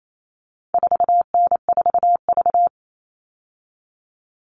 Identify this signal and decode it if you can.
Morse code, heard as a keyed tone, about 28 wpm, 700 Hz. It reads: 4N4V
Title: All Cut Numbers - 4-Digit 28wpm